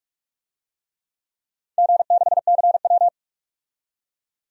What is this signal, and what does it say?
Morse code, heard as a keyed tone, about 46 wpm, 700 Hz. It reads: G6CW
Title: DXCC-100 Call Signs - 4 Characters - 45wpm